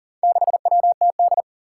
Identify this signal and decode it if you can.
Morse code, heard as a keyed tone, about 40 wpm, 700 Hz. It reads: 6WTD
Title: Four Challenging Letter-Number Combinations 40wpm